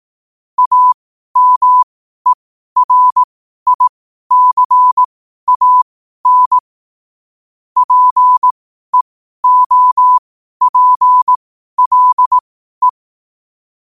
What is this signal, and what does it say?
Morse code, heard as a keyed tone, about 18 wpm, spaced as for 12 wpm, 1000 Hz. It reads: AMERICAN PEOPLE
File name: qotdfiles / 12wpm